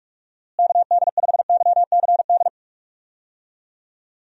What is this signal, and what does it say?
Morse code, heard as a keyed tone, about 42 wpm, 700 Hz. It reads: KD5YCD
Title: USA Call Signs - 6 Characters - Repeated 2x - 45wpm